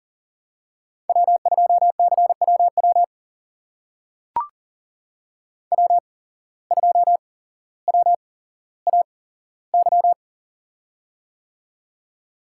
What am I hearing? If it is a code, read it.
Morse code, heard as a keyed tone, about 40 wpm, spaced as for 17 wpm, 700 Hz. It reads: W2CWW W2WAY
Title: USA Call Signs - 5 Characters - 8x Character Spacing - 40wpm